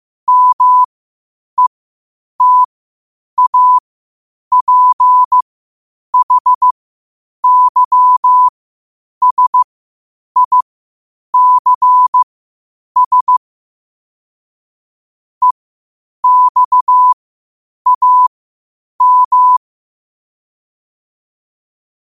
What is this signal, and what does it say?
Morse code, heard as a keyed tone, about 15 wpm, spaced as for 8 wpm, 1000 Hz. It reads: METAPHYSICS EXAM
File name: qotdfiles / 8wpm